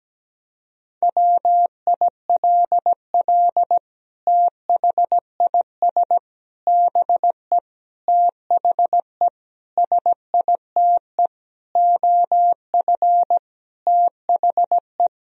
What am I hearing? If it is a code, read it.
Morse code, heard as a keyed tone, about 17 wpm, 700 Hz. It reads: WILL THIS BE THE SITE OF THE
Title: Sentences from Top 1000 Words - Review Part 1 - 17wpm